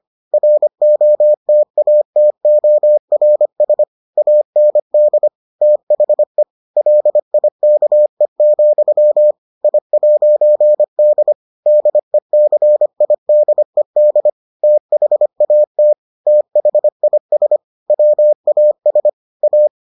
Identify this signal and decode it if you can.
Morse code, heard as a keyed tone, about 25 wpm, 600 Hz. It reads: ROTATORS AND THE LIKE, I'D DECIDED THAT THIS WAS A